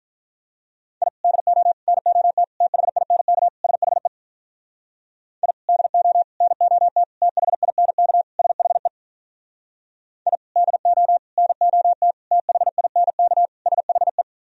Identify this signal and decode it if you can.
Morse code, heard as a keyed tone, about 46 wpm, 700 Hz. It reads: I DO NOT THINK SHE I DO NOT THINK SHE I DO NOT THINK SHE